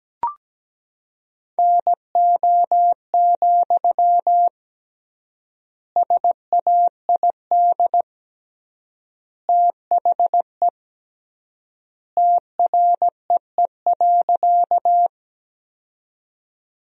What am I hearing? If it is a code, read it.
Morse code, heard as a keyed tone, about 17 wpm, 700 Hz. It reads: NO, SAID THE TREE.
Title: 3x Word-Spacing - Sentences from Top 300 Words - 17wpm